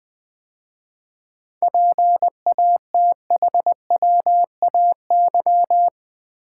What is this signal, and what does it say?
Morse code, heard as a keyed tone, about 20 wpm, 700 Hz. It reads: PATHWAY
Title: 7 Letter Words - First 3 Letters Given - 20wpm